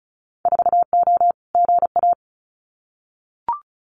Morse code, heard as a keyed tone, about 34 wpm, 700 Hz. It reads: 4O GA
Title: Winter Field Day Exchanges 35wpm